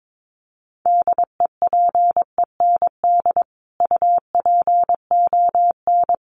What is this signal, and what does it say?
Morse code, heard as a keyed tone, about 22 wpm, 700 Hz. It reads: DEPEND UPON
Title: Sentences from Top 1300 Words 22wpm